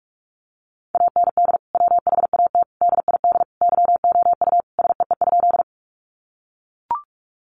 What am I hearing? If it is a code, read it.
Morse code, heard as a keyed tone, about 44 wpm, 700 Hz. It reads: AND WHAT DID YOU SEE?
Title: Sentences from Top 100 Words 45wpm